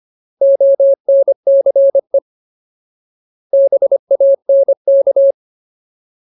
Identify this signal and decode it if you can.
Morse code, heard as a keyed tone, about 25 wpm, 550 Hz. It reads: ONCE BANK
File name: Ultimate Word List / 25WPM Ultimate